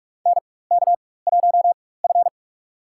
Morse code, heard as a keyed tone, about 44 wpm, spaced as for 21 wpm, 700 Hz. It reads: NK1F
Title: USA Call Signs - 4 Characters - 4x Character Spacing - 45wpm